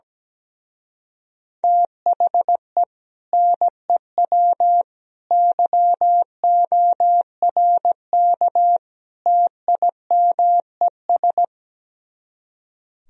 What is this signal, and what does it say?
Morse code, heard as a keyed tone, about 17 wpm, 700 Hz. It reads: THE NEW YORK TIMES
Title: Speed-Racing - Sets of 4 Words 17wpm